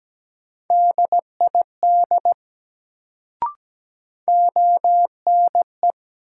Morse code, heard as a keyed tone, about 17 wpm, 700 Hz. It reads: DID ONE
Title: Sets of 2 Words - New Word Drill - 17wpm